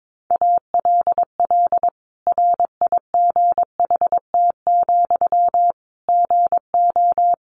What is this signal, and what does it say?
Morse code, heard as a keyed tone, about 22 wpm, 700 Hz. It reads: ALL RIGHT, GO
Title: Sentences from Top 1100 Words 22wpm